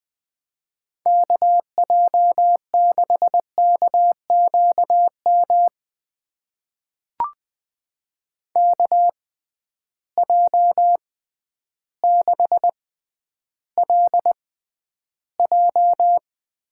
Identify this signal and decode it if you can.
Morse code, heard as a keyed tone, about 20 wpm, spaced as for 11 wpm, 700 Hz. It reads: KJ6KQM KJ6LJ
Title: USA Call Signs - 6 Characters - 6x Character Spacing - 20wpm